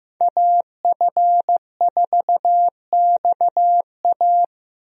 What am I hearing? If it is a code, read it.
Morse code, heard as a keyed tone, about 15 wpm, 700 Hz. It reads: AF4XA
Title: USA Call Signs - 5 Characters - Repeated 2x - 15wpm